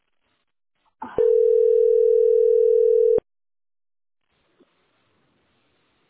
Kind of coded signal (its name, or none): telephone call progress tone